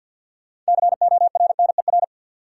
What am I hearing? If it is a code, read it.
Morse code, heard as a keyed tone, about 53 wpm, 700 Hz. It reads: CORNER
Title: Sets of 5 Words - New Word Drill - 50wpm